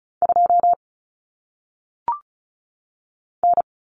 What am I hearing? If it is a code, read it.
Morse code, heard as a keyed tone, about 36 wpm, 700 Hz. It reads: 2 N
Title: Common Cut Numbers - 1-Digit 35wpm